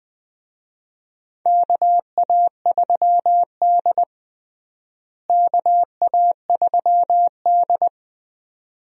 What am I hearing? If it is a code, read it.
Morse code, heard as a keyed tone, about 20 wpm, 700 Hz. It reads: KA3D KA3D
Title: USA Call Signs - 4 Characters - Repeated 2x - 20wpm